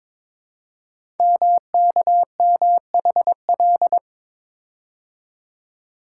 Morse code, heard as a keyed tone, about 22 wpm, 700 Hz. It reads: MKMHL
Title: Code Groups - 5 Letters - 22wpm